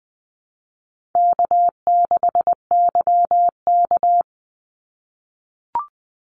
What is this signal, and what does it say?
Morse code, heard as a keyed tone, about 20 wpm, 700 Hz. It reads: K6YK
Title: NA SOTA Chaser Callsigns 20wpm